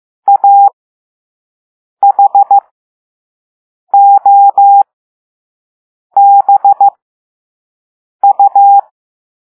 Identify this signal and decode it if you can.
Morse code, heard as a keyed tone, about 15 wpm, spaced as for 5 wpm, 800 Hz. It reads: AHOBU